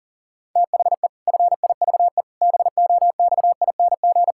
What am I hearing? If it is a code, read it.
Morse code, heard as a keyed tone, about 40 wpm, 700 Hz. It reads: THE FIVE BOXING
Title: Pangram Challenge v2 - 40wpm